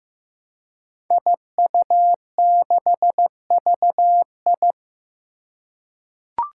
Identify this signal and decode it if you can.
Morse code, heard as a keyed tone, about 15 wpm, 700 Hz. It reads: IU6VI
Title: Five Challenging Letter-Number Combinations 15wpm